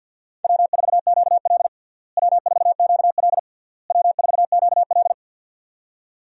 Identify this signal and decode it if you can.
Morse code, heard as a keyed tone, about 52 wpm, 700 Hz. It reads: W4QL W4QL W4QL
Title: USA Call Signs - 4 Characters - Repeated 3x - 50wpm